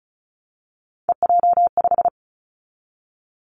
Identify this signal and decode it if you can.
Morse code, heard as a keyed tone, about 36 wpm, 700 Hz. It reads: EJ5
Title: Three Challenging Letter-Number Combinations 35wpm